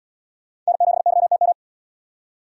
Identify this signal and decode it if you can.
Morse code, heard as a keyed tone, about 75 wpm, 700 Hz. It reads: N7YTM